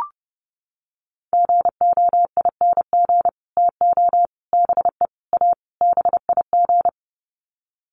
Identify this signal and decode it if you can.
Morse code, heard as a keyed tone, about 30 wpm, 700 Hz. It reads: GOING TO BE A BIG
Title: Sets of 5 Words Encore2 30wpm